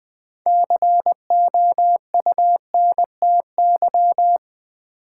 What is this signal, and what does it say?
Morse code, heard as a keyed tone, about 20 wpm, 700 Hz. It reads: COUNTY
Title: Six Letter Words - 20wpm